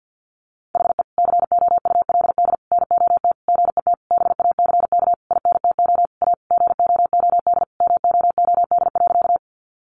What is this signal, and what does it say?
Morse code, heard as a keyed tone, about 50 wpm, 700 Hz. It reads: HE COULD NOT GET BACK INTO A GOOD MOOD.
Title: Sentences from Top 1100 Words 50wpm